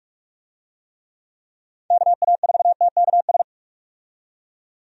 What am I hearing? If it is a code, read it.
Morse code, heard as a keyed tone, about 46 wpm, 700 Hz. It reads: KA4TKS
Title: USA Call Signs - 6 Characters - 45wpm